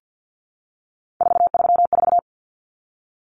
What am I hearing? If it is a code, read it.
Morse code, heard as a keyed tone, about 50 wpm, 700 Hz. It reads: 434
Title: Common Cut Numbers - 3-Digit 50wpm